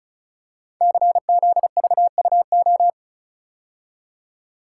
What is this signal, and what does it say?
Morse code, heard as a keyed tone, about 34 wpm, 700 Hz. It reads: CZVUO
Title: Code Groups - 5 Letter:Numbers - 35wpm